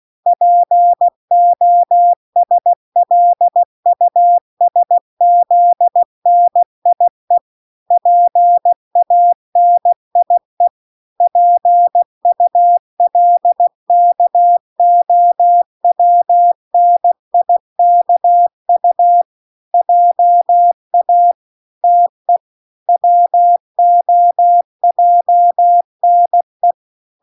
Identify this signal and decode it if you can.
Morse code, heard as a keyed tone, about 16 wpm, 700 Hz. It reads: POSLUSZNIE PANIE PULKOWNIKU JA TE WOJNE